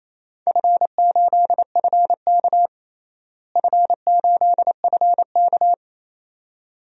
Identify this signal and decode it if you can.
Morse code, heard as a keyed tone, about 28 wpm, 700 Hz. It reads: F8FK F8FK
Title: DXCC-100 Call Signs - 4 Characters - Repeated 2x - 28wpm